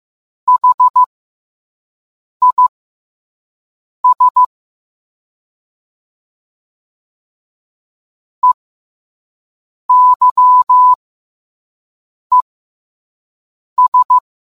Morse code, heard as a keyed tone, about 15 wpm, spaced as for 5 wpm, 1000 Hz. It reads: HIS EYES